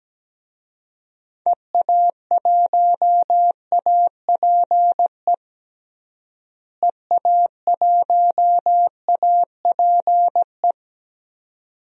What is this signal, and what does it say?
Morse code, heard as a keyed tone, about 17 wpm, 700 Hz. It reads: EA1APE EA1APE
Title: DXCC-100 Call Signs - 6 Characters - Repeated 2x - 17wpm